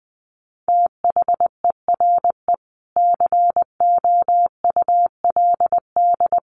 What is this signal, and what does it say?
Morse code, heard as a keyed tone, about 20 wpm, 700 Hz. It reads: THERE COULD
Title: Sentences from Top 600 Words 20wpm